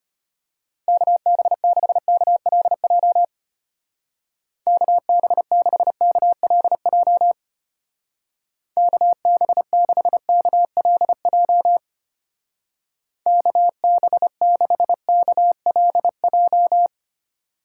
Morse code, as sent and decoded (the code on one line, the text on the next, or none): -.- -... -.... -.- .-.. .--- / -.- -... -.... -.- .-.. .--- / -.- -... -.... -.- .-.. .--- / -.- -... -.... -.- .-.. .---
KB6KLJ KB6KLJ KB6KLJ KB6KLJ